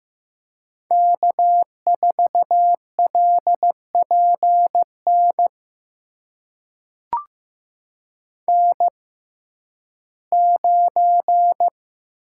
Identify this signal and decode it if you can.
Morse code, heard as a keyed tone, about 15 wpm, spaced as for 11 wpm, 700 Hz. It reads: K4LPN N9
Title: USA Call Signs - 5 Characters - 6x Character Spacing - 15wpm